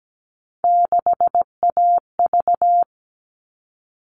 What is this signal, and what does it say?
Morse code, heard as a keyed tone, about 17 wpm, 700 Hz. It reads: 6AV
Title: All Cut Numbers - 3-Digit 17wpm